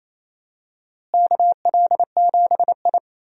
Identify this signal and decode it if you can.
Morse code, heard as a keyed tone, about 28 wpm, 700 Hz. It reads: KL7I
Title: DXCC-100 Call Signs - 4 Characters - Repeated 3x - 28wpm